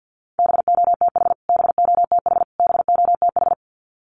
Morse code, heard as a keyed tone, about 50 wpm, 700 Hz. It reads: BOTH BOTH BOTH
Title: Top 500 Words Repeated 50wpm